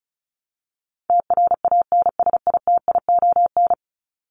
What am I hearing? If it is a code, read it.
Morse code, heard as a keyed tone, about 34 wpm, 700 Hz. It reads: TRANSITION
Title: Top 1751-2000 Words 35wpm